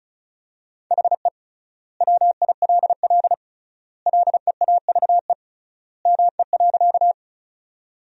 Morse code, heard as a keyed tone, about 36 wpm, 700 Hz. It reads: HE WILL LEAVE ME.
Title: 3x Word-Spacing - Sentences from Top 400 Words - 35wpm